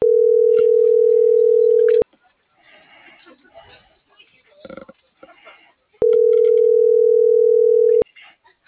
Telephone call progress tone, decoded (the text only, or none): ringback tone